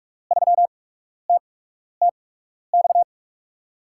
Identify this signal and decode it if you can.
Morse code, heard as a keyed tone, about 41 wpm, spaced as for 12 wpm, 700 Hz. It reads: 3TTX
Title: USA Call Signs - 6 Characters - 8x Character Spacing - 45wpm